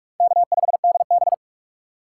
Morse code, heard as a keyed tone, about 43 wpm, 700 Hz. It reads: K5DB
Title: USA Call Signs - 6x Character Spacing - 45wpm